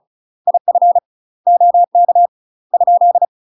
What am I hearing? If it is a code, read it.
Morse code, heard as a keyed tone, about 36 wpm, 700 Hz. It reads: IF OK ?
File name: UK MASTER QSOs TRAINING / 35WPM / UK MASTER QSOs 0012